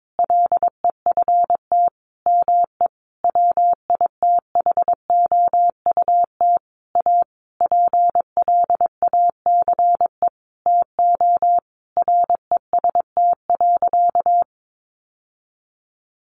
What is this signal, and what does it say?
Morse code, heard as a keyed tone, about 22 wpm, 700 Hz. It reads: LEFT ME WITHOUT A PLACE TO REST.
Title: Sentences from Top 600 Words 22wpm